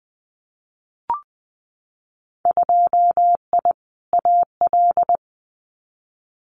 Morse code, heard as a keyed tone, about 19 wpm, 700 Hz. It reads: 2I AL
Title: Winter Field Day Exchanges 20wpm